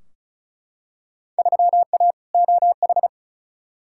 Morse code, heard as a keyed tone, about 36 wpm, 700 Hz. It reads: 3A OH
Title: Field Day Exchanges - 35wpm